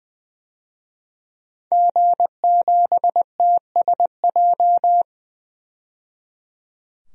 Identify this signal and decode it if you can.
Morse code, heard as a keyed tone, about 20 wpm, 700 Hz. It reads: G7TSJ
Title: DXCC-100 Call Signs - 5 Characters - 20wpm